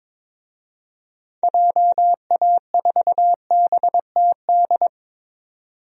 Morse code, heard as a keyed tone, about 22 wpm, 700 Hz. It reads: JA4BTD
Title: DXCC-100 Call Signs - 6 Characters - 22wpm